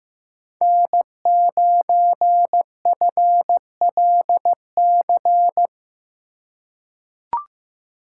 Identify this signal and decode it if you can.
Morse code, heard as a keyed tone, about 15 wpm, 700 Hz. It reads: N9FLC
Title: USA Call Signs - 2x Character Spacing - 15wpm